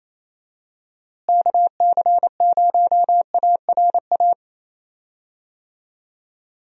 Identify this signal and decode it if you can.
Morse code, heard as a keyed tone, about 28 wpm, 700 Hz. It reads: KC0ARA
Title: USA Call Signs - 6 Characters - Repeated 2x - 28wpm